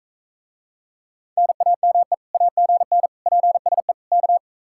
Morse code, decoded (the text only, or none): NAME AGN PSE K